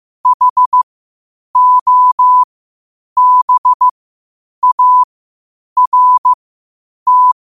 Morse code, heard as a keyed tone, about 15 wpm, spaced as for 8 wpm, 1000 Hz. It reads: HOBART
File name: qotdfiles / 8wpm